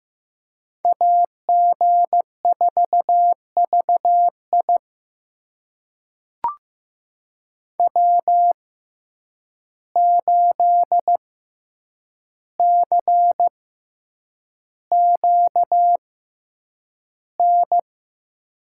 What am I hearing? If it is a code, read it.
Morse code, heard as a keyed tone, about 15 wpm, spaced as for 8 wpm, 700 Hz. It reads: AG4VI W8CQN
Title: USA Call Signs - 5 Characters - 6x Character Spacing - 15wpm